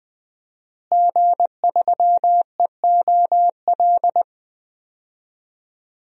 Morse code, heard as a keyed tone, about 20 wpm, 700 Hz. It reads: G3EOL
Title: DXCC-100 Call Signs - 5 Characters - Repeated 3x - 20wpm